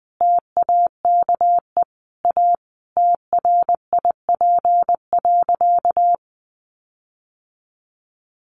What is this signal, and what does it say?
Morse code, heard as a keyed tone, about 20 wpm, 700 Hz. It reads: TAKE A TRIP.
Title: Sentences from Top 1300 Words 20wpm